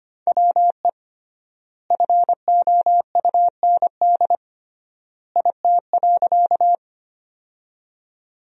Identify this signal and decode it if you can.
Morse code, heard as a keyed tone, about 25 wpm, 700 Hz. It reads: WE FOUND IT.
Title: 3x Word-Spacing - Sentences from Top 300 Words - 25wpm